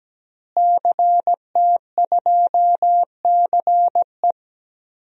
Morse code, heard as a keyed tone, about 17 wpm, 700 Hz. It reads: CT2CE
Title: DXCC-100 Call Signs - 5 Characters - Repeated 3x - 17wpm